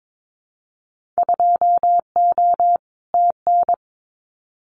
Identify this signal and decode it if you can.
Morse code, heard as a keyed tone, about 22 wpm, 700 Hz. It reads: 2O TN
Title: Winter Field Day Exchanges 22wpm